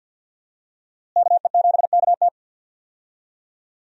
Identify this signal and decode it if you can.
Morse code, heard as a keyed tone, about 48 wpm, 700 Hz. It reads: KE7KT
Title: USA Call Signs - 5 Characters - Repeated 3x - 50wpm